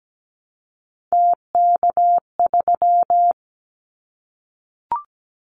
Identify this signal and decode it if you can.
Morse code, heard as a keyed tone, about 17 wpm, 700 Hz. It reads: TK3
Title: Callsign Prefixes for Top 100 DXCC - 17wpm